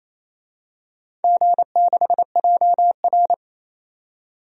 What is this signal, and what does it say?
Morse code, heard as a keyed tone, about 28 wpm, 700 Hz. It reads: G6JR
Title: DXCC-100 Call Signs - 4 Characters - 28wpm